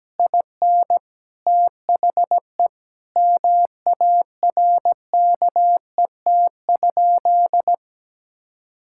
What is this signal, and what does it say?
Morse code, heard as a keyed tone, about 17 wpm, 700 Hz. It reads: IN THE MARKET?
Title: Sentences from Top 1000 Words - Review Part 6 - 17wpm